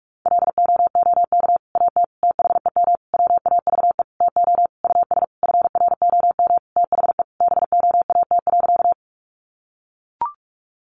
Morse code, heard as a keyed tone, about 44 wpm, 700 Hz. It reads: LOOK AT THEM WAVE TO US FROM THE BOAT.
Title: Sentences from Top 1400 Words 45wpm